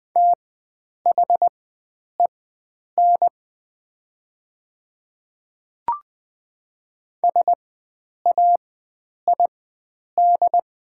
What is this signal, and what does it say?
Morse code, heard as a keyed tone, about 20 wpm, spaced as for 9 wpm, 700 Hz. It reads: THEN SAID
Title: 4x Character Spacing - Top 100 Words - 20wpm